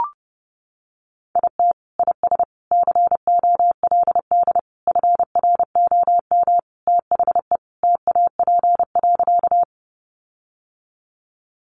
Morse code, heard as a keyed tone, about 30 wpm, 700 Hz. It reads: IT IS COLD FROM THE TAP.
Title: Sentences from Top 1400 Words 30wpm